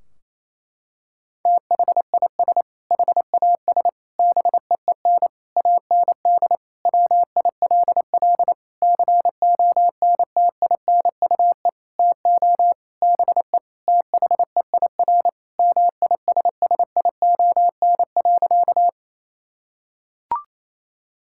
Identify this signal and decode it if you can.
Morse code, heard as a keyed tone, about 28 wpm, 700 Hz. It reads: THIS HAS BEEN AND WILL CONTINUE TO BE THEIR MISSION.
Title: Sentences from Top 1600 Words 28wpm